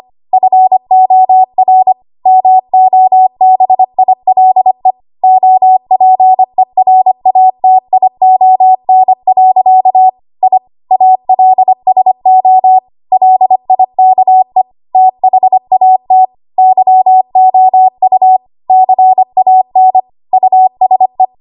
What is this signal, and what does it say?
Morse code, heard as a keyed tone, about 25 wpm, 750 Hz. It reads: FOR MOBILE OPERATION. I ALSO LIKE THAT YOU CAN USE
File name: Archive / 25 WPM / 25WPM